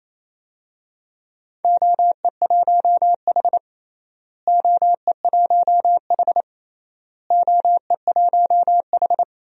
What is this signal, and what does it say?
Morse code, heard as a keyed tone, about 28 wpm, 700 Hz. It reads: OE1H OE1H OE1H
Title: DXCC-100 Call Signs - 4 Characters - Repeated 3x - 28wpm